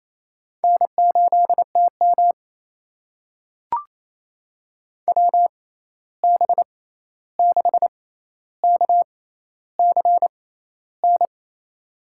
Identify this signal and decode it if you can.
Morse code, heard as a keyed tone, about 28 wpm, spaced as for 13 wpm, 700 Hz. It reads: N8TM WB6KCN
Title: USA Call Signs - 6x Character Spacing - 28wpm